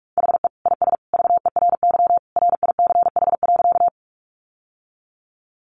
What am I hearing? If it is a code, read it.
Morse code, heard as a keyed tone, about 44 wpm, 700 Hz. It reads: HE IS VERY RICH.
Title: Sentences from Top 1100 Words 45wpm